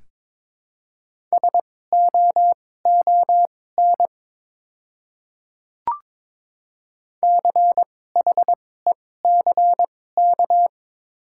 Morse code, heard as a keyed tone, about 22 wpm, spaced as for 16 wpm, 700 Hz. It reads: SOON CHECK
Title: 2x Character Spacing - Top 500 Words - 22wpm